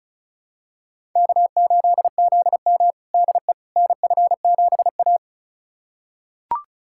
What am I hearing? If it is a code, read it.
Morse code, heard as a keyed tone, about 36 wpm, 700 Hz. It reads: K8ZM DE NF7A
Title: USA Callsign DE Callsign - 4 Characters - 35wpm